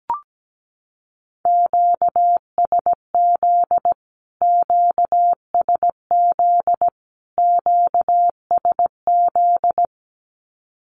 Morse code, heard as a keyed tone, about 17 wpm, 700 Hz. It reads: QSZ QSZ QSZ
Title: Q-Codes Repeated 3x 17wpm